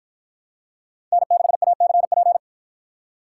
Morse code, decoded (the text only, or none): N6ACP